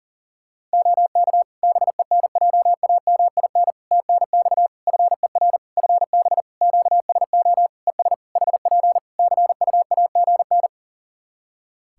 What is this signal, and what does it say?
Morse code, heard as a keyed tone, about 38 wpm, 700 Hz. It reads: OK BENJAMIN TNX FER FB QSO ES HP CUAGN
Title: Practice QSO - Exchange 4 - Ending - 40wpm